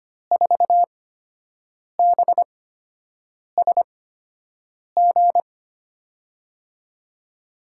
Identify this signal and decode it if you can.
Morse code, heard as a keyed tone, about 25 wpm, spaced as for 7 wpm, 700 Hz. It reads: 4BSG
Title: USA Call Signs - 6 Characters - 8x Character Spacing - 25wpm